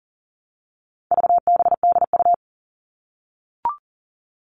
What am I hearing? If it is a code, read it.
Morse code, heard as a keyed tone, about 40 wpm, 700 Hz. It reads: VBDU
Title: All Cut Numbers - 4-Digit 40wpm